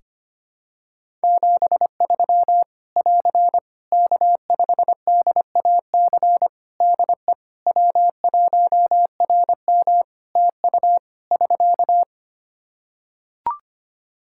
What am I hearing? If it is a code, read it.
Morse code, heard as a keyed tone, about 25 wpm, 700 Hz. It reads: 73 <AR> K5DAC DE W1RM TU <SK>
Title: Practice QSO - Exchange 4 - Ending - 25wpm